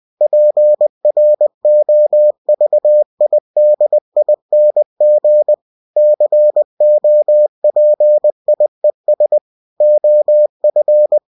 Morse code, heard as a keyed tone, about 20 wpm, 600 Hz. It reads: PROVIDING COPIES OF
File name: cw-books / winnie-the-pooh / winnie-20wpm-page-0120